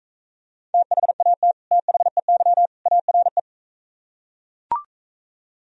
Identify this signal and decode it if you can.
Morse code, heard as a keyed tone, about 42 wpm, 700 Hz. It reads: THAT THEY ARE